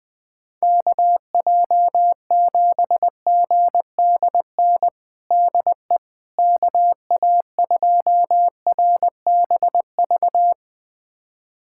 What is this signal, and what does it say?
Morse code, heard as a keyed tone, about 20 wpm, 700 Hz. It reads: KJ7GDN DE KA2RBV
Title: USA Callsign DE Callsign - 6 Characters - 20wpm